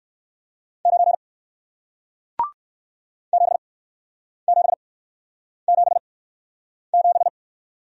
Morse code, heard as a keyed tone, about 61 wpm, spaced as for 9 wpm, 700 Hz. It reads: 9 7777